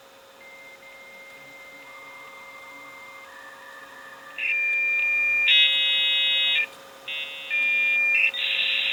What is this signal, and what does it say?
Dial-up modem handshake.
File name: IMG / WEB / 2023.08.01 56k dialup modem sound